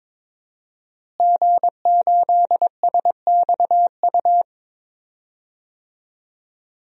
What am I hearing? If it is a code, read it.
Morse code, heard as a keyed tone, about 22 wpm, 700 Hz. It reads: G8SXU
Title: DXCC-100 Call Signs - 5 Characters - Repeated 2x - 22wpm